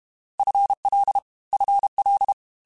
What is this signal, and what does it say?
Morse code, heard as a keyed tone, about 32 wpm, 800 Hz. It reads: FL FL
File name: FL / FL mp3s / FL Prefixes 3x2 Twice @30